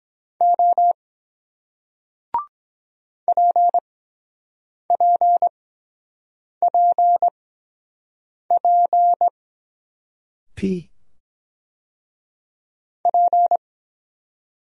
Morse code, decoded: O P P P P P

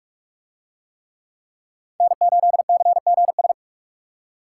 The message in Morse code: N8CGS

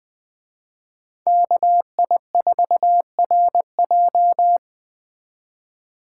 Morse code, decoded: KI4RJ

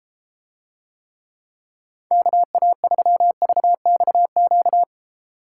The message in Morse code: KA3VXQ

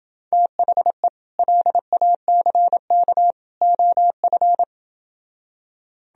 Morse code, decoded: THE LACK OF

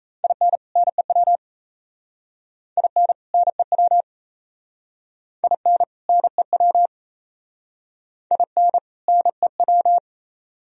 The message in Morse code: IN NEW IN NEW IN NEW IN NEW

